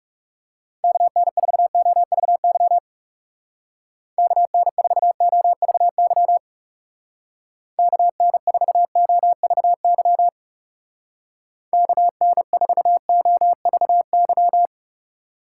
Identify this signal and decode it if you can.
Morse code, heard as a keyed tone, about 36 wpm, 700 Hz. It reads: KN4OVY KN4OVY KN4OVY KN4OVY
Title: Speed Racing - USA 2x3 Call Signs 30wpm